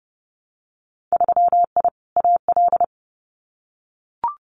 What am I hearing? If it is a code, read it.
Morse code, heard as a keyed tone, about 30 wpm, 700 Hz. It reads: 3I AL
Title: Winter Field Day Exchanges Encore 30wpm